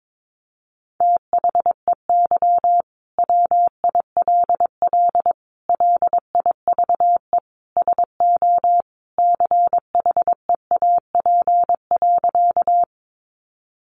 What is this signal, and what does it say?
Morse code, heard as a keyed tone, about 22 wpm, 700 Hz. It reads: THEY WILL LIVE SO CHEAP.
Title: Sentences from Top 1200 Words 22wpm